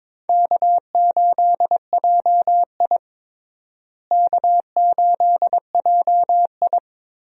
Morse code, heard as a keyed tone, about 21 wpm, 700 Hz. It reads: K8JI K8JI
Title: USA Call Signs - 4 Characters - Repeated 3x - 22wpm